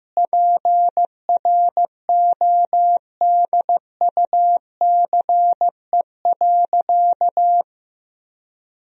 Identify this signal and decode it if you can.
Morse code, heard as a keyed tone, about 15 wpm, 700 Hz. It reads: PRODUCE.
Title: Sentences from Top 1000 Words - Review Part 3 - 15wpm